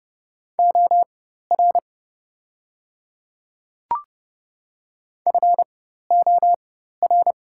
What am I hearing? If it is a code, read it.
Morse code, heard as a keyed tone, about 30 wpm, spaced as for 14 wpm, 690 Hz. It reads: OR FOR
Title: 4x Character Spacing - Top 500 Words - 30wpm